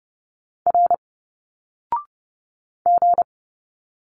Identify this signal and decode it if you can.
Morse code, heard as a keyed tone, about 30 wpm, 700 Hz. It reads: R G
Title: Single Letters 30wpm